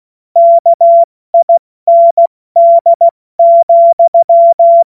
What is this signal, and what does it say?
Morse code, heard as a keyed tone, about 16 wpm, spaced as for 14 wpm, 680 Hz. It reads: KIND,
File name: Morsekurs / Witze / Witz 16. November 2021, 70 F80